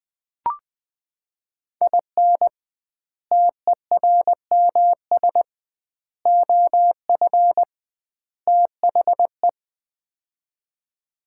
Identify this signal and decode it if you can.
Morse code, heard as a keyed tone, about 20 wpm, 700 Hz. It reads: IN TERMS OF THE
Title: 2x Word-Spacing - Sets of 4 Words - 20wpm